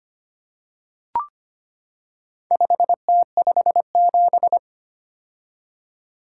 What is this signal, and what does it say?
Morse code, heard as a keyed tone, about 25 wpm, 700 Hz. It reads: 5T57
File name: practice / 25wpm / Four Letter-Number Combinations - 25wpm